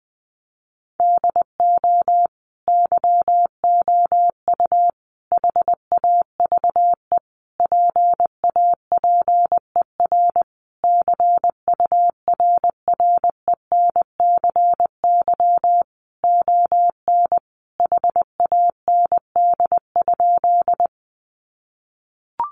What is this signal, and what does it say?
Morse code, heard as a keyed tone, about 20 wpm, 700 Hz. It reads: DO YOU HAVE PAPER CURRENCY ON HAND?
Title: Sentences from Top 1500 Words 20wpm